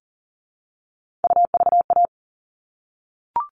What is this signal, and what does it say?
Morse code, heard as a keyed tone, about 40 wpm, 700 Hz. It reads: UVA
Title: All Cut Numbers - 3-Digit 40wpm